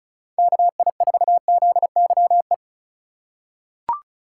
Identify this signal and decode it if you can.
Morse code, heard as a keyed tone, about 36 wpm, 700 Hz. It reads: KI4ZYE
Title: USA Call Signs - 6 Characters - 2x Character Spacing - 35wpm